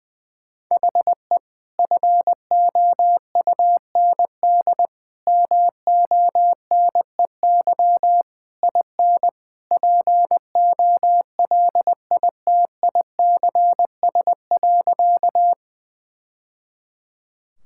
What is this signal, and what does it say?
Morse code, heard as a keyed tone, about 20 wpm, 700 Hz. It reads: HE FOUND MONEY IN POLITICS.
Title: Sentences from Top 1000 Words - Review Part 6 - 20wpm